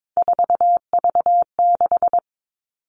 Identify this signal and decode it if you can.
Morse code, heard as a keyed tone, about 22 wpm, 700 Hz. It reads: 4V6
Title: All Cut Numbers - 3-Digit 22wpm